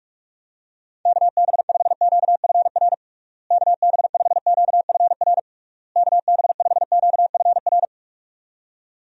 Morse code, heard as a keyed tone, about 42 wpm, 700 Hz. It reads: KB5QFR KB5QFR KB5QFR
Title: USA Call Signs - 6 Characters - Repeated 3x - 45wpm